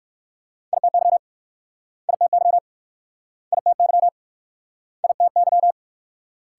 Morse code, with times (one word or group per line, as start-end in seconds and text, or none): 0.7-1.2 ITY
2.1-2.6 ITY
3.5-4.1 ITY
5.0-5.7 ITY